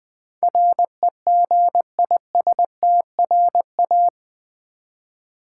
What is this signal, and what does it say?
Morse code, heard as a keyed tone, about 20 wpm, 700 Hz. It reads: REGISTRA
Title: Build A Word - 4 Letters At A Time - 12-Letter Words - 20wpm